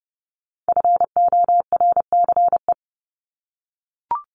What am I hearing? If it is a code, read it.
Morse code, heard as a keyed tone, about 30 wpm, 700 Hz. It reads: FORCE
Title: ED Suffix - Top 100 Words with Base Words 30wpm